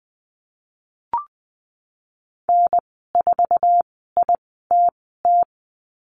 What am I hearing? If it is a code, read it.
Morse code, heard as a keyed tone, about 20 wpm, spaced as for 14 wpm, 700 Hz. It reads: N4ITT
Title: 055 Lesson 10g - Letter r - Call Signs 20wpm